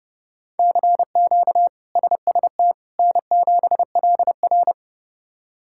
Morse code, heard as a keyed tone, about 30 wpm, 700 Hz. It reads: CQ SST N7LR
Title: K1USN SST QSO Practice - 30wpm